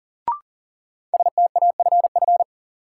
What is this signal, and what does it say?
Morse code, heard as a keyed tone, about 40 wpm, 700 Hz. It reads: STAFF
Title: Sentences from Top 1200 Words - New Word Drill - 40wpm